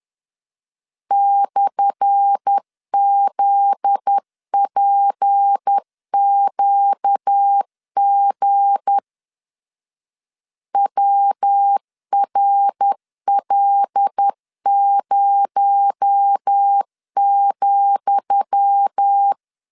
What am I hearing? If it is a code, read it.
Morse code, heard as a keyed tone, about 10 wpm, 800 Hz. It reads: /ZPQG WRL0,
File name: LEZIONI / 10 SIGNS / 10D COMA 10wpm 3013